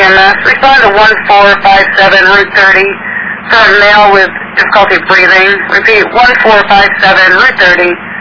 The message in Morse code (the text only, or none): none